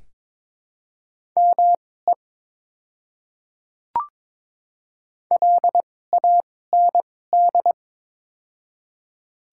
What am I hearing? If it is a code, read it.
Morse code, heard as a keyed tone, about 22 wpm, spaced as for 16 wpm, 700 Hz. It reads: ME LAND
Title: 2x Character Spacing - Top 300 Words - 22wpm